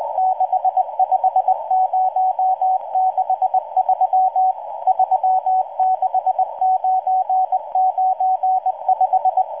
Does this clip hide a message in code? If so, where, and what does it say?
Yes: Morse code from 0.2 to 9.4 s, heard as a keyed tone, about 19 wpm, 750 Hz, with radio noise. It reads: 65063 36995